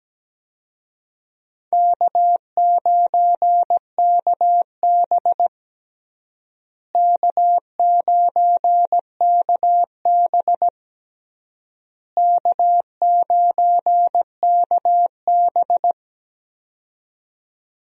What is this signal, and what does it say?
Morse code, heard as a keyed tone, about 17 wpm, 700 Hz. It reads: K9KB K9KB K9KB
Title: USA Call Signs - 4 Characters - Repeated 3x - 17wpm